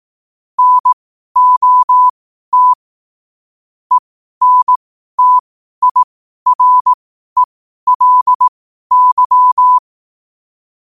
Morse code, heard as a keyed tone, about 18 wpm, spaced as for 12 wpm, 1000 Hz. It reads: NOT ENTIRELY
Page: Quotes for Wed, 13 Aug 2025 in Morse Code at 12 words per minute.